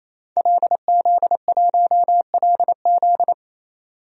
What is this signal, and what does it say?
Morse code, heard as a keyed tone, about 28 wpm, 700 Hz. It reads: LZ1LZ
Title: DXCC-100 Call Signs - 5 Characters - 28wpm